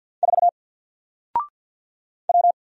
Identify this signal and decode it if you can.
Morse code, heard as a keyed tone, about 48 wpm, 700 Hz. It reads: 4 W
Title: Single Letter-Number Alt 50wpm